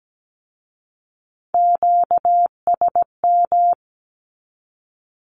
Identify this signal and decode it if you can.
Morse code, heard as a keyed tone, about 17 wpm, 700 Hz. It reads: QSM
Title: Q-Codes with Spoken Meaning - 17wpm